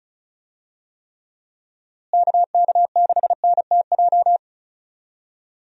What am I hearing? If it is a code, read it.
Morse code, heard as a keyed tone, about 36 wpm, 700 Hz. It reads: KK6NTJ
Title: USA Call Signs - 6 Characters - Repeated 3x - 35wpm